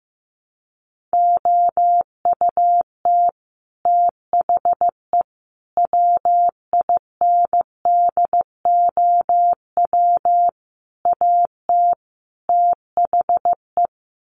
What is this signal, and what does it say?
Morse code, heard as a keyed tone, about 15 wpm, 700 Hz. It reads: OUT THE WINDOW AT THE
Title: Sets of 5 Words Encore2 15wpm